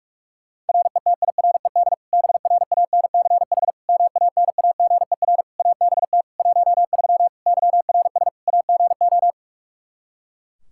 Morse code, heard as a keyed tone, about 42 wpm, 700 Hz. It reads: RETIRED BRANCH MANAGER ABT 13 YRS AGO
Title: K1USN SST QSO Practice 45wpm